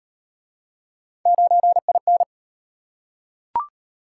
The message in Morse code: 9IN

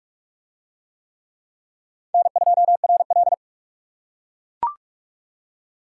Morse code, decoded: N2RL

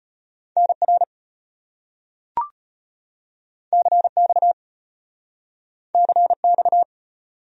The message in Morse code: NR CX CX

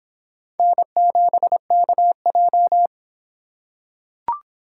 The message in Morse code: N7KJ

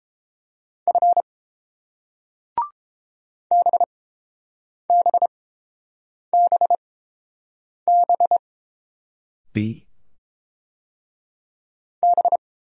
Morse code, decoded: F BBBB B

